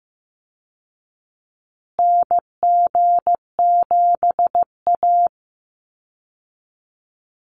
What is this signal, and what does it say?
Morse code, heard as a keyed tone, about 15 wpm, 700 Hz. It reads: NG7A
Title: Call Signs Repeated 3x - 15wpm